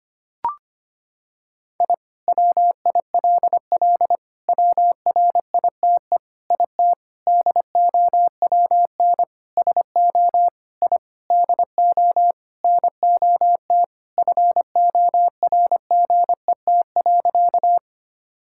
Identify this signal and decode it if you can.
Morse code, heard as a keyed tone, about 25 wpm, 700 Hz. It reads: I WILL WRITE IT DOWN SO I DO NOT FORGET.
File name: practice / 25wpm / Sentences with 4-Word Phrases - 25wpm